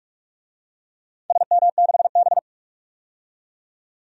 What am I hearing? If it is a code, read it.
Morse code, heard as a keyed tone, about 46 wpm, 700 Hz. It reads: SM6B
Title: DXCC-100 Call Signs - 4 Characters - 45wpm